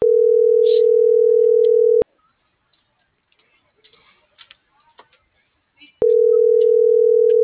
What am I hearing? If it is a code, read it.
Ringback tone (North America), 440 + 480 Hz.